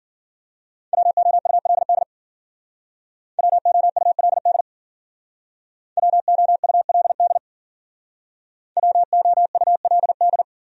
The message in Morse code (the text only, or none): WOULD WOULD WOULD WOULD